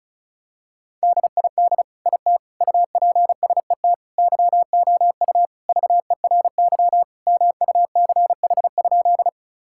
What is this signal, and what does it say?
Morse code, heard as a keyed tone, about 36 wpm, 700 Hz. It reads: DID IT UPSET YOU VERY MUCH?
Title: Sentences from Top 1000 Words - Review Part 5 - 35wpm